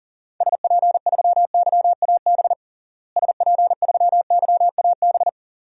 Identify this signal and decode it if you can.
Morse code, heard as a keyed tone, about 40 wpm, 700 Hz. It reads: SP3YAB SP3YAB
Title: DXCC-100 Call Signs - 6 Characters - Repeated 2x - 40wpm